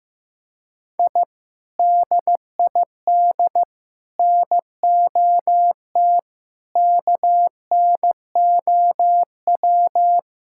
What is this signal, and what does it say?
Morse code, heard as a keyed tone, about 15 wpm, 700 Hz. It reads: I DID NOT KNOW
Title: Sentences from Top 1000 Words - Review Part 6 - 15wpm